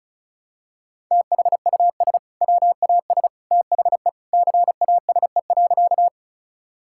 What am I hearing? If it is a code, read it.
Morse code, heard as a keyed tone, about 36 wpm, 700 Hz. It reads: THUS WAS THE CASE.
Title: Sentences from Top 1000 Words - Review Part 5 - 35wpm